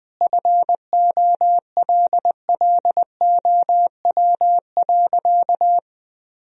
Morse code, heard as a keyed tone, about 20 wpm, 700 Hz. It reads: FOLLOW.
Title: Sentences from Top 1000 Words - Review Part 2 - 20wpm